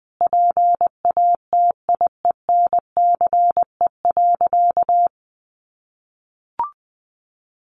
Morse code, heard as a keyed tone, about 20 wpm, 700 Hz. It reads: PATIENCE.
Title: Sentences from Top 1300 Words 20wpm